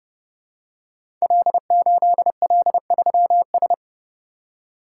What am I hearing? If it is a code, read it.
Morse code, heard as a keyed tone, about 29 wpm, 700 Hz. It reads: L8L3S
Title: Code Groups - 5 Letter:Numbers - 30wpm